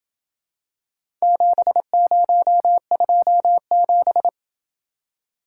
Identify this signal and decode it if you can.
Morse code, heard as a keyed tone, about 27 wpm, 700 Hz. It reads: 7027